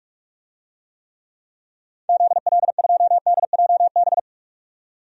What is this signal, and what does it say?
Morse code, heard as a keyed tone, about 47 wpm, 700 Hz. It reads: ZL2DJB
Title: DXCC-100 Call Signs - 6 Characters - 45wpm